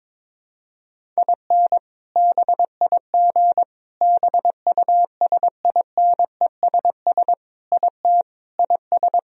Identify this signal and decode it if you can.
Morse code, heard as a keyed tone, about 22 wpm, 700 Hz. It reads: IN BIG BUSINESS IT IS
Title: Sentences from Top 1000 Words - Review Part 1 - 22wpm